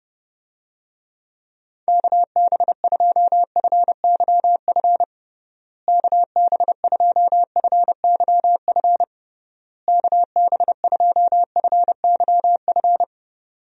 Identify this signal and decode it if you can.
Morse code, heard as a keyed tone, about 30 wpm, 700 Hz. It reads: KB2FYF KB2FYF KB2FYF
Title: USA Call Signs - 6 Characters - Repeated 3x - 30wpm